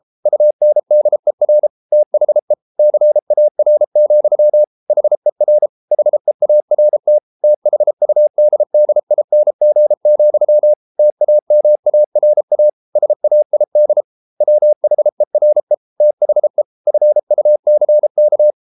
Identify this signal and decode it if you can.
Morse code, heard as a keyed tone, about 33 wpm, 600 Hz. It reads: UNDER THE CAR, HER HEART THUDDING, TAMARA SAID WHERE THE FUCK